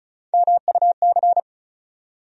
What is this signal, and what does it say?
Morse code, heard as a keyed tone, about 34 wpm, 700 Hz. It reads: MUC
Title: USA Call Sign Postfixes - 3 Letters 35wpm